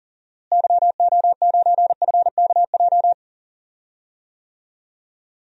Morse code, heard as a keyed tone, about 40 wpm, 700 Hz. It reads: YO9FKJ
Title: DXCC-100 Call Signs - 6 Characters - Repeated 2x - 40wpm